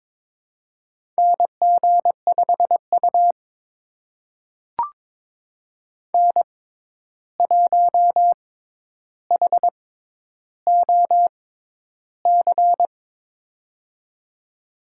Morse code, heard as a keyed tone, about 22 wpm, spaced as for 11 wpm, 700 Hz. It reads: NG5U N1HOC
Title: USA Call Signs - 6x Character Spacing - 22wpm